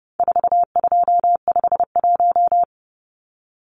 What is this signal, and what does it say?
Morse code, heard as a keyed tone, about 29 wpm, 700 Hz. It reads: 4251
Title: Numbers - 4-Digit 30wpm